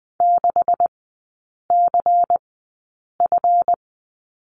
Morse code, heard as a keyed tone, about 20 wpm, spaced as for 8 wpm, 700 Hz. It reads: 6CF
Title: 138 Lesson 23e - Letter f - Three Characters 20wpm